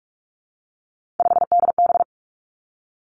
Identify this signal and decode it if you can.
Morse code, heard as a keyed tone, about 46 wpm, 700 Hz. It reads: 5DB
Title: Three Challenging Letter-Number Combinations 45wpm